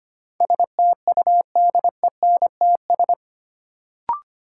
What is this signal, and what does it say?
Morse code, heard as a keyed tone, about 25 wpm, 700 Hz. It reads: STUDENTS
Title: Sets of 5 Words Encore - New Word Drill - 25wpm